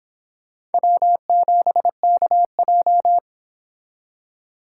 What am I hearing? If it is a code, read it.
Morse code, heard as a keyed tone, about 31 wpm, 700 Hz. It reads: W7KJ